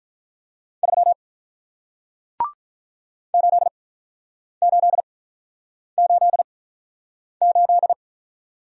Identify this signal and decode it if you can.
Morse code, heard as a keyed tone, about 48 wpm, spaced as for 9 wpm, 700 Hz. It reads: 3 8888